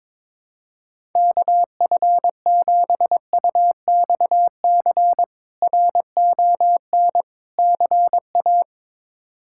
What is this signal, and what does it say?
Morse code, heard as a keyed tone, about 21 wpm, 700 Hz. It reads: KF7UXC RON CA
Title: K1USN SST QSO Practice - 22wpm